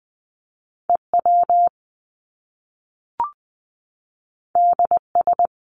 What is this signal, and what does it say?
Morse code, heard as a keyed tone, about 20 wpm, 700 Hz. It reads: EW DS
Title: Top Two Letter Combinations in Words 20wpm